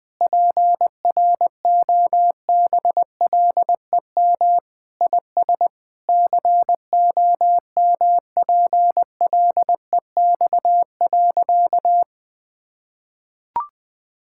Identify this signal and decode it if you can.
Morse code, heard as a keyed tone, about 20 wpm, 700 Hz. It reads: PROBLEM IS COMPLEX.
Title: Sentences with 4-Word Phrases - Spoken Beginning - 20wpm